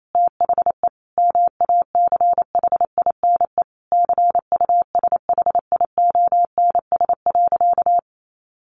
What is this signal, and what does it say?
Morse code, heard as a keyed tone, about 28 wpm, 700 Hz. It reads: THE MACHINE CUSHIONS.
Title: A Sound Of Thunder 28wpm